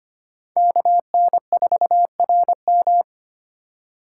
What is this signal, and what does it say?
Morse code, heard as a keyed tone, about 25 wpm, 700 Hz. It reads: KN4RM
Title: USA Call Signs - 5 Characters - 2x Character Spacing - 25wpm